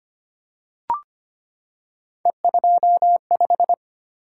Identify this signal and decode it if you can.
Morse code, heard as a keyed tone, about 25 wpm, 700 Hz. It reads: E25
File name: Three Letter-Number Combinations - 25wpm